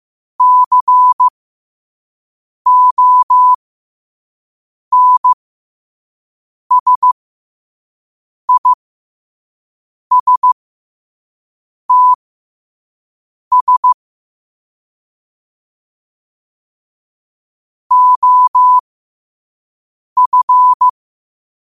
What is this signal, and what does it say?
Morse code, heard as a keyed tone, about 15 wpm, spaced as for 5 wpm, 1000 Hz. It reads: CONSISTS OF
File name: qotdfiles / 5wpm